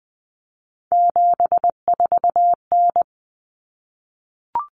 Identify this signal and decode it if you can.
Morse code, heard as a keyed tone, about 20 wpm, 700 Hz. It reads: 74N
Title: Common Cut Numbers - 3-Digit 20wpm